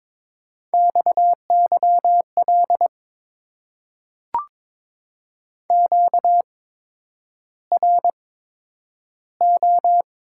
Morse code, heard as a keyed tone, about 22 wpm, spaced as for 9 wpm, 700 Hz. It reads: XYL QRO
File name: Top 90 QSO Elements - 8x Character Spacing - 22wpm